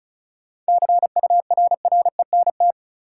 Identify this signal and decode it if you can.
Morse code, heard as a keyed tone, about 36 wpm, 700 Hz. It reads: CURRENT
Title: Sentences from Top 1600 Words - New Word Drill Alt - 35wpm